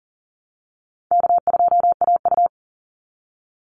Morse code, heard as a keyed tone, about 40 wpm, 700 Hz. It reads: K2AU
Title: Call Signs Repeated 3x - 40wpm